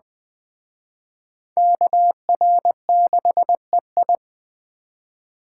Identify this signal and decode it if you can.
Morse code, heard as a keyed tone, about 21 wpm, 700 Hz. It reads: KR6EI